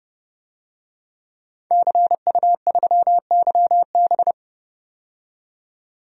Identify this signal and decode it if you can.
Morse code, heard as a keyed tone, about 30 wpm, 700 Hz. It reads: CU3YB
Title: DXCC-100 Call Signs - 5 Characters - 30wpm